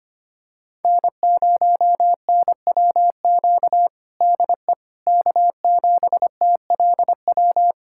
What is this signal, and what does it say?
Morse code, heard as a keyed tone, about 25 wpm, 700 Hz. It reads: N0NWQ DE K7TLW
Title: USA Callsign DE Callsign - 5 Characters - 25wpm